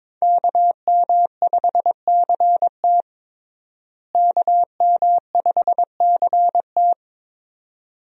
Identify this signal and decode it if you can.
Morse code, heard as a keyed tone, about 21 wpm, 700 Hz. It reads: KM5CT KM5CT
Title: USA Call Signs - 5 Characters - Repeated 2x - 22wpm